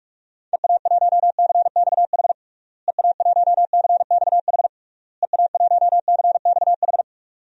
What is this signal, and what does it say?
Morse code, heard as a keyed tone, about 47 wpm, 700 Hz. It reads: EA1CXH EA1CXH EA1CXH
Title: DXCC-100 Call Signs - 6 Characters - Repeated 3x - 45wpm